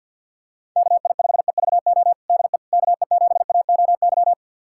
Morse code, heard as a keyed tone, about 47 wpm, 700 Hz. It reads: KI5EVO DE KE7AOY
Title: USA Callsign DE Callsign 50wpm